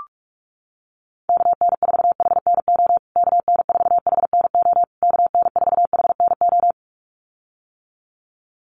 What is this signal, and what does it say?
Morse code, heard as a keyed tone, about 44 wpm, 700 Hz. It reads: KN4HNO KN4HNO KN4HNO
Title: USA Call Signs Repeated 3x 45wpm